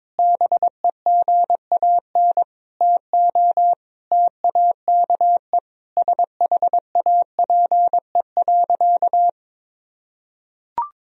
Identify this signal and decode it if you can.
Morse code, heard as a keyed tone, about 21 wpm, 700 Hz. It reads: BEGAN TO TAKE SHAPE.
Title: Sentences from Top 1000 Words - Review Part 3 - 22wpm